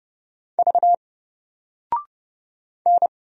Morse code, heard as a keyed tone, about 30 wpm, 700 Hz. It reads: V N
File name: Single-Letter-Number-30wpm